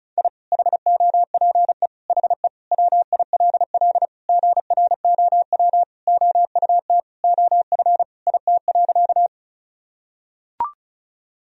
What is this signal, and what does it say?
Morse code, heard as a keyed tone, about 34 wpm, 700 Hz. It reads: I HOPE HE WILL GROW OUT OF IT.
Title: Sentences from Top 1000 Words - Review Part 1 - 35wpm